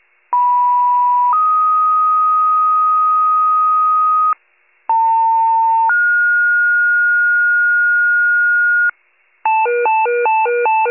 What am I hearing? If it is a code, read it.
Two-tone page, heard as a one-second tone, then a three-second tone, about 980 then 1250 Hz, about 900 then 1430 Hz, with noise. It is heard from 0.3 to 8.9 s.
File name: pager / 91 FIRE / wcalert